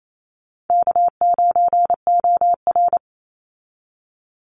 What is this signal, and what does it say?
Morse code, heard as a keyed tone, about 28 wpm, 700 Hz. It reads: K9OR
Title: Easy Call Signs Repeated 2x - 28wpm